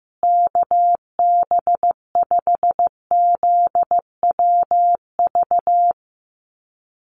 Morse code, heard as a keyed tone, about 15 wpm, 700 Hz. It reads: KB5ZWV
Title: USA Call Signs Repeated 3x 15wpm